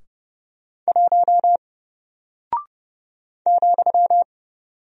Morse code, heard as a keyed tone, about 30 wpm, 700 Hz. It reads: 1 ,
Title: Single Character (Letter:Number:Punctuation:Prosign) Alt 30wpm